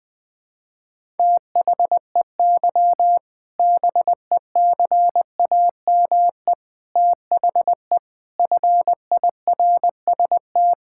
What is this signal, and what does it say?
Morse code, heard as a keyed tone, about 21 wpm, 700 Hz. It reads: THEY BECAME THE FIRST
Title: Sentences with 8-Word Phrases - 20wpm